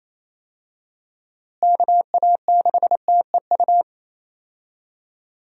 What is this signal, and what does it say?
Morse code, heard as a keyed tone, about 28 wpm, 700 Hz. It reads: KA6TEU
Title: USA Call Signs - 6 Characters - Repeated 2x - 28wpm